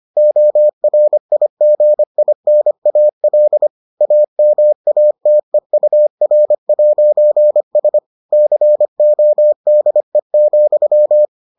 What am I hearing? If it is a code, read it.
Morse code, heard as a keyed tone, about 25 wpm, 600 Hz. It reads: ORIGINAL AMATEUR'S CODE,